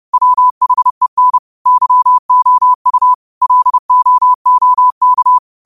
Morse code, heard as a keyed tone, about 30 wpm, 1000 Hz. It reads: WHEN YOU LOOK
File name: qotdfiles / 30wpm